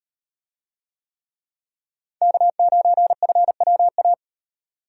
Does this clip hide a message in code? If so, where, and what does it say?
Yes: Morse code from 2.2 to 4.1 s, heard as a keyed tone, about 36 wpm, 700 Hz. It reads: K9FWA